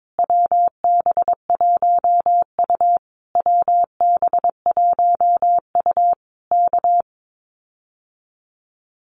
Morse code, heard as a keyed tone, about 22 wpm, 700 Hz. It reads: WB1U WB1U K
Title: Common QSO Phrases Encore 22wpm